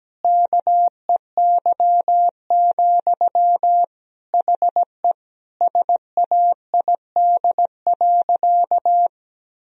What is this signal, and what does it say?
Morse code, heard as a keyed tone, about 17 wpm, 700 Hz. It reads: KEY, HE SAID.
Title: Sentences from Top 1000 Words - Review Part 4 - 17wpm